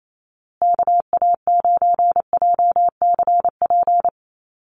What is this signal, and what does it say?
Morse code, heard as a keyed tone, about 28 wpm, 700 Hz. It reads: KA9JCP
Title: USA Call Signs Repeated 2x 28wpm